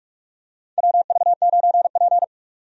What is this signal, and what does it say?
Morse code, heard as a keyed tone, about 46 wpm, 700 Hz. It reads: WV9P
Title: USA Call Signs - 4 Characters - 45wpm